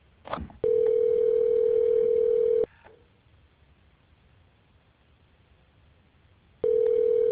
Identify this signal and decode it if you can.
Ringback tone (North America), 440 + 480 Hz.